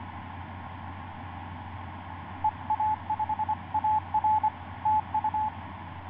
Morse code, heard as a keyed tone, about 25 wpm, 860 Hz. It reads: EA5AR TU
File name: 5i3a ik2fiq etc 12m cw 09feb2010 1151